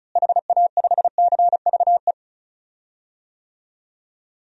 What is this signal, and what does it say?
Morse code, heard as a keyed tone, about 34 wpm, 700 Hz. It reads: HA5CVE
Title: DXCC-100 Call Signs - 6 Characters - Repeated 2x - 35wpm